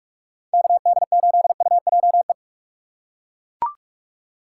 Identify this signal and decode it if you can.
Morse code, heard as a keyed tone, about 41 wpm, 700 Hz. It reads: KD8UJE
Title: USA Call Signs - 6 Characters - 2x Character Spacing - 45wpm